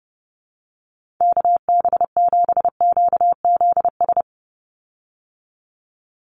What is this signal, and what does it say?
Morse code, heard as a keyed tone, about 30 wpm, 700 Hz. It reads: KB7QZS
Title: USA Call Signs 30wpm